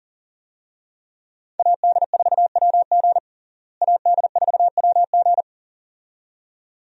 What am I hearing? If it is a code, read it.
Morse code, heard as a keyed tone, about 40 wpm, 700 Hz. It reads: AD4WG AD4WG
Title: USA Call Signs - 5 Characters - Repeated 2x - 40wpm